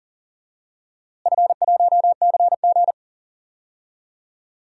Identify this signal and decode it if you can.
Morse code, heard as a keyed tone, about 40 wpm, 700 Hz. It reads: F1CG
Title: DXCC-100 Call Signs - 4 Characters - Repeated 2x - 40wpm